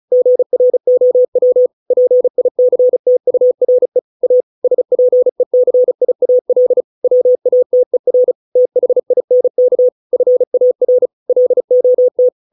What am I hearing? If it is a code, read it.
Morse code, heard as a keyed tone, about 36 wpm, 500 Hz. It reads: GROW PICTURE A SPECIAL WATER THINK FAR LOT